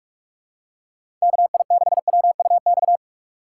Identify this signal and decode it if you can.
Morse code, heard as a keyed tone, about 42 wpm, 700 Hz. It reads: KI6WUX
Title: USA Call Signs - 3x Character Spacing - 45wpm